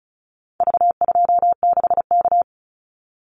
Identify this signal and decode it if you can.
Morse code, heard as a keyed tone, about 34 wpm, 700 Hz. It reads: V26K
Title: Call Signs Repeated 2x - 35wpm